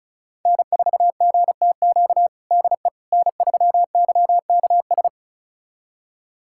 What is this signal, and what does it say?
Morse code, heard as a keyed tone, about 34 wpm, 690 Hz. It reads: N4GTQ DE N3YKS
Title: USA Callsign DE Callsign - 5 Characters - 35wpm